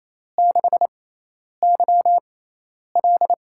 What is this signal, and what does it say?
Morse code, heard as a keyed tone, about 28 wpm, spaced as for 10 wpm, 700 Hz. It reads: 6YL
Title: USA Call Signs - 4 Characters - 6x Character Spacing - 28wpm